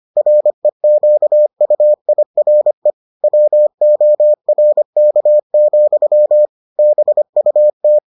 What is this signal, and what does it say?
Morse code, heard as a keyed tone, about 25 wpm, 600 Hz. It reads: REQUIRE WORK, BUT